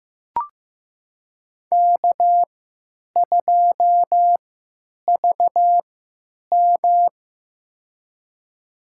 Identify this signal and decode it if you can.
Morse code, heard as a keyed tone, about 15 wpm, spaced as for 9 wpm, 700 Hz. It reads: K2VM
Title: USA Call Signs - 4 Characters - 3x Character Spacing - 15wpm